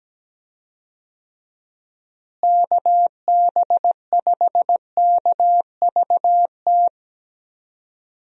Morse code, heard as a keyed tone, about 17 wpm, 700 Hz. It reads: KB5KVT
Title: USA Call Signs - 6 Characters - Repeated 3x - 17wpm